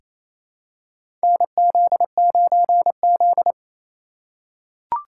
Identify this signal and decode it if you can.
Morse code, heard as a keyed tone, about 28 wpm, 700 Hz. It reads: NZ9Z
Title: USA Call Signs - 2x Character Spacing - 28wpm